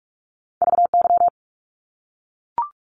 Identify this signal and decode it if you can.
Morse code, heard as a keyed tone, about 44 wpm, 700 Hz. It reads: VY
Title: Abbreviations Repeated 3x - 45wpm